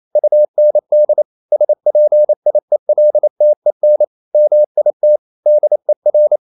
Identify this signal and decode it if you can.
Morse code, heard as a keyed tone, about 28 wpm, 600 Hz. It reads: UND SPIELTEN MIT DER
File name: marienkind9 28wpm